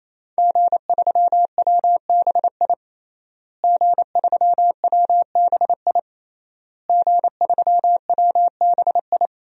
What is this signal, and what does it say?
Morse code, heard as a keyed tone, about 28 wpm, 700 Hz. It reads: G3WBI G3WBI G3WBI
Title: DXCC-100 Call Signs - 5 Characters - Repeated 3x - 28wpm